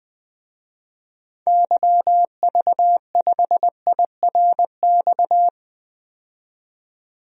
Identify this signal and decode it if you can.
Morse code, heard as a keyed tone, about 20 wpm, 700 Hz. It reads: YV5IRX
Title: DXCC-100 Call Signs - 6 Characters - 20wpm